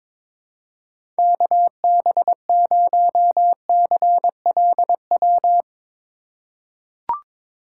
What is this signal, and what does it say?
Morse code, heard as a keyed tone, about 21 wpm, 700 Hz. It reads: KB0CLW
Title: USA Call Signs - 6 Characters - 3x Character Spacing - 22wpm